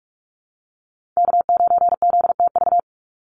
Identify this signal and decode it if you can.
Morse code, heard as a keyed tone, about 44 wpm, 700 Hz. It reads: K9ZTV
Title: Call Signs Repeated 2x - 45wpm